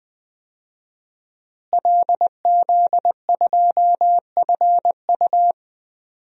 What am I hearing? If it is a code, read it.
Morse code, heard as a keyed tone, about 20 wpm, 700 Hz. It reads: LZ2FU
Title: DXCC-100 Call Signs - 5 Characters - Repeated 2x - 20wpm